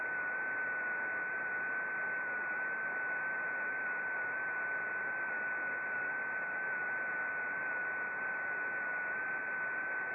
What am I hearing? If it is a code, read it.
FT8.
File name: contests / sept-vhf-k5tr- -0300